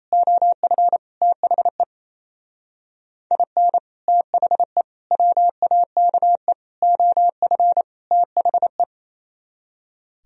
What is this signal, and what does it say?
Morse code, heard as a keyed tone, about 35 wpm, 700 Hz. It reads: OF THE IN THE WAKE OF THE